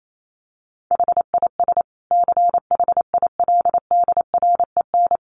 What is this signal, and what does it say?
Morse code, heard as a keyed tone, about 28 wpm, 700 Hz. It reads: HIS CHILDREN
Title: Sentences from Top 900 Words 28wpm